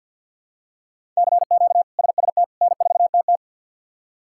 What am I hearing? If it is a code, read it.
Morse code, heard as a keyed tone, about 52 wpm, 700 Hz. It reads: CQ SST N4TT
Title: K1USN SST QSO Practice - 50wpm